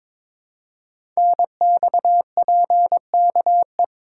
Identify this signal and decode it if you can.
Morse code, heard as a keyed tone, about 22 wpm, 700 Hz. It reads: NXPKE
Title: Code Groups - 5 Letters - 22wpm